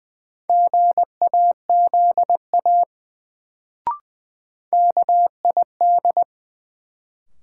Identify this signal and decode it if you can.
Morse code, heard as a keyed tone, about 20 wpm, 700 Hz. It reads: GAZA KID
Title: Sets of 5 Words - New Word Drill - 20wpm